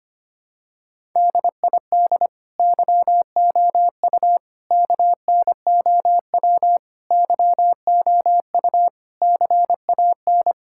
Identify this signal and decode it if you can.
Morse code, heard as a keyed tone, about 25 wpm, 700 Hz. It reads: DID YOU KNOW YOU CAN
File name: Sets of 9 Words - 25wpm